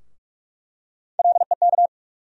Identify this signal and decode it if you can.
Morse code, heard as a keyed tone, about 42 wpm, 700 Hz. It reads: LEK
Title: USA Call Sign Postfixes - 3 Letters 45wpm